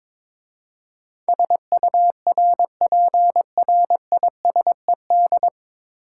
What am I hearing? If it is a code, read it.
Morse code, heard as a keyed tone, about 22 wpm, 700 Hz. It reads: SURPRISED
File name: 9 Letter Words - First 3 Letters Given - 22wpm